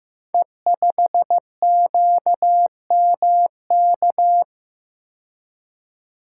Morse code, heard as a keyed tone, about 15 wpm, 700 Hz. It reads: E5QMK
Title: Code Groups - 5 Letter:Numbers - 15wpm